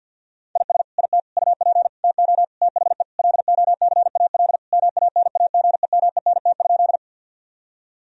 Morse code, heard as a keyed tone, about 50 wpm, 700 Hz. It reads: IS IT UP TO THE LOCAL MANAGEMENT?
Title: Sentences from Top 1000 Words - Review Part 4 - 50wpm